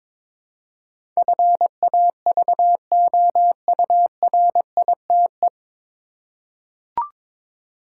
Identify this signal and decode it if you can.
Morse code, heard as a keyed tone, about 22 wpm, 700 Hz. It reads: FAVOURITE
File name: 9 Letter Words - First 3 Letters Given - 22wpm